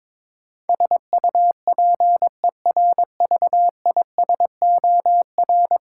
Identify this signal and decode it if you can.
Morse code, heard as a keyed tone, about 22 wpm, 700 Hz. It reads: SUPERVISOR
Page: morse-code-ninja-practice-22wpm